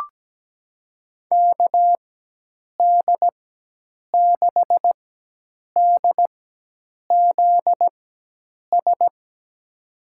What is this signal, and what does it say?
Morse code, heard as a keyed tone, about 17 wpm, spaced as for 8 wpm, 700 Hz. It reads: KD6DZS
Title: USA Call Signs - 6 Characters - 4x Character Spacing - 17wpm